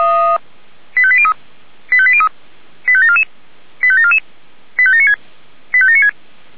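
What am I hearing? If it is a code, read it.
Five-tone selective call (ZVEI), heard as quick runs of five tones, tones of about 70 ms. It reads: 85803 85803 85744 85744 85796 85796
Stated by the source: ZVEI five-tone selective call received by SDR, feuerwehr-orlamuende.org.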